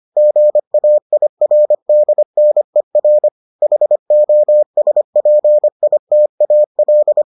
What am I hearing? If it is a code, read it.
Morse code, heard as a keyed tone, about 25 wpm, 600 Hz. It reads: GAIRDNER HOSPITAL